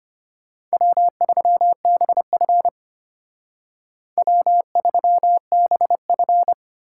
Morse code, heard as a keyed tone, about 32 wpm, 700 Hz. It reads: W3BF W3BF